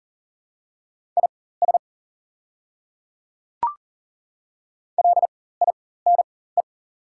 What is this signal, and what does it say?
Morse code, heard as a keyed tone, about 40 wpm, spaced as for 19 wpm, 690 Hz. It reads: IS LINE
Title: 4x Character Spacing - Top 300 Words - 40wpm